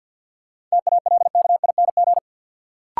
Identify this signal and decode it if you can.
Morse code, heard as a keyed tone, about 46 wpm, 700 Hz. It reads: TALKING
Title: Sets of 2 Words - New Word Drill - 50wpm